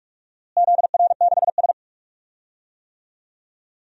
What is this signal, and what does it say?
Morse code, heard as a keyed tone, about 46 wpm, 700 Hz. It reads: ZR6S
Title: DXCC-100 Call Signs - 4 Characters - Repeated 3x - 45wpm